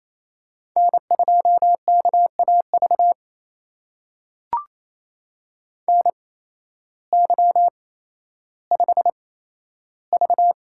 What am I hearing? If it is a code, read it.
Morse code, heard as a keyed tone, about 28 wpm, spaced as for 13 wpm, 700 Hz. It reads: N2KAV NY5V
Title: USA Call Signs - 8x Character Spacing - 28wpm